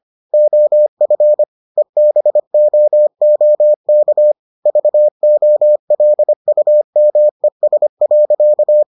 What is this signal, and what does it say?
Morse code, heard as a keyed tone, about 25 wpm, 600 Hz. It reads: OF EBOOK VOLUMES.